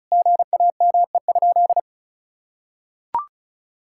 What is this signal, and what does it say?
Morse code, heard as a keyed tone, about 34 wpm, 700 Hz. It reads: GAME?
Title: Sentences from Top 1000 Words - Review Part 5 - 35wpm